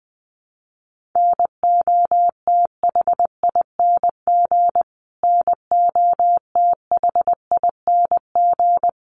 Morse code, heard as a keyed tone, about 20 wpm, 700 Hz. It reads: NOTHING NOTHING
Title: Top 500 Words Repeated 20wpm